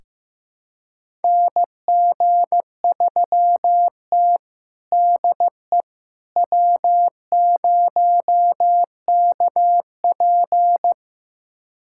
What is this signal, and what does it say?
Morse code, heard as a keyed tone, about 15 wpm, 700 Hz. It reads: NG3T DE W0KP
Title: USA Callsign DE Callsign - 4 Characters - 15wpm